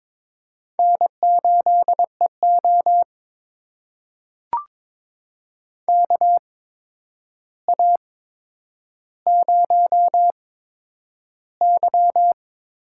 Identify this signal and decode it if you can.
Morse code, heard as a keyed tone, about 22 wpm, spaced as for 9 wpm, 700 Hz. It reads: N8EO KA0Y
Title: USA Call Signs - 8x Character Spacing - 22wpm